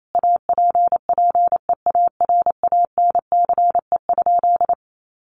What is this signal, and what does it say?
Morse code, heard as a keyed tone, about 28 wpm, 700 Hz. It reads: APPEARANCE?
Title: Sentences from Top 1100 Words 28wpm